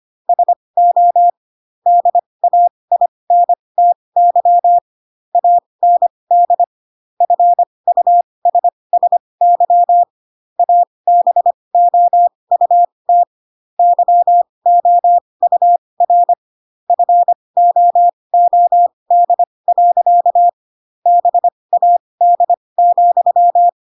Morse code, heard as a keyed tone, about 25 wpm, spaced as for 20 wpm, 700 Hz. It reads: SO DAINTY AND FUSSY ABOUT YOUR FOOD. BAD,
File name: Pinocchio / Pinocchio ch7 15 25 ik1hge